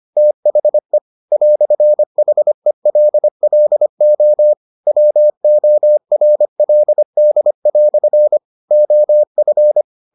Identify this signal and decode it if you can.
Morse code, heard as a keyed tone, about 25 wpm, 600 Hz. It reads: THE "HELLO WORLD" OF